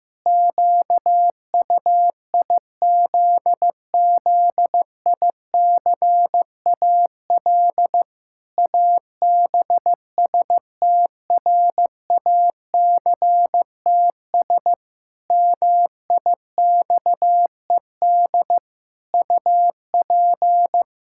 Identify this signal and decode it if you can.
Morse code, heard as a keyed tone, about 15 wpm, 700 Hz. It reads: QUIZZICAL ABSTRACTS MIXED UP
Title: Pangram Challenge v2 - 15wpm